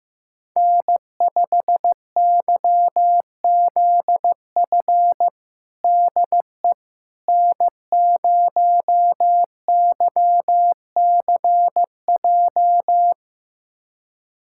Morse code, heard as a keyed tone, about 15 wpm, 700 Hz. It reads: N5YZF DE N0YCJ
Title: USA Callsign DE Callsign - 5 Characters - 15wpm